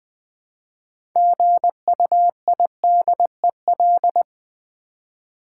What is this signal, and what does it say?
Morse code, heard as a keyed tone, about 20 wpm, 700 Hz. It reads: GUIDEL
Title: Build A Word - 3 Letters At A Time - 9-Letter Words - 20wpm